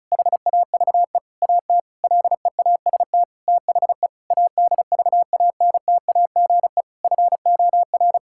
Morse code, heard as a keyed tone, about 36 wpm, 700 Hz. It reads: HAVE AT LEAST THE ADVANTAGE FOR
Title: Sentences from Top 1000 Words - Review Part 6 - 35wpm